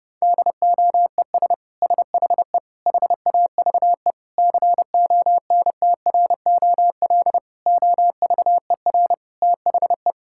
Morse code, heard as a keyed tone, about 30 wpm, 700 Hz. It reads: DOES SHE HAVE CONTROL OVER THE
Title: Sentences from Top 1000 Words - Review Part 5 - 30wpm